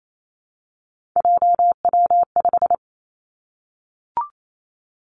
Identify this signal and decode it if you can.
Morse code, heard as a keyed tone, about 28 wpm, 700 Hz. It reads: JW5
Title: Three Challenging Letter-Number Combinations 28wpm